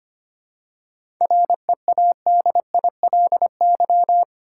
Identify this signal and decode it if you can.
Morse code, heard as a keyed tone, about 25 wpm, 700 Hz. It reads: READILY
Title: Seven Letter Words - 25wpm